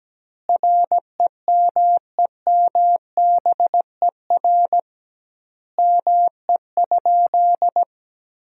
Morse code, heard as a keyed tone, about 17 wpm, 700 Hz. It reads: REMEMBER ME?
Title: 2x Word-Spacing - Sentences from Top 400 Words - 17wpm